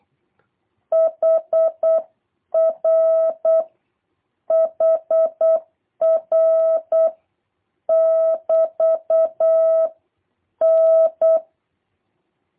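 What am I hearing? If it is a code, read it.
Morse code, heard as a keyed tone, about 8 wpm, 650 Hz. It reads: HR HR <BT> N
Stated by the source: Morse copy drill by JH8CBH: 24 random characters, A, E, N, T six times each, 3-second gaps for seven questions, 2-second gaps for three.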